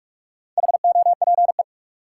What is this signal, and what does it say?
Morse code, heard as a keyed tone, about 43 wpm, 700 Hz. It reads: HOPE
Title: Sentences from Top 1000 Words - New Word Drill - 45wpm